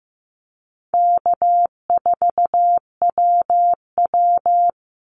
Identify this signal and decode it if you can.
Morse code, heard as a keyed tone, about 15 wpm, 700 Hz. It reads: K4WW
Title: Easy Call Signs Repeated 3x - 15wpm